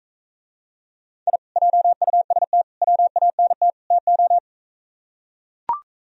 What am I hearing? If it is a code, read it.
Morse code, heard as a keyed tone, about 43 wpm, 700 Hz. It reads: I JUST WANT TO